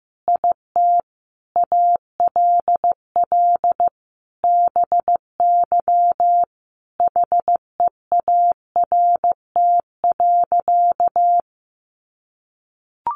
Morse code, heard as a keyed tone, about 15 wpm, 700 Hz. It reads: IT ALL BY HEART.
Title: Sentences from Top 1200 Words 15wpm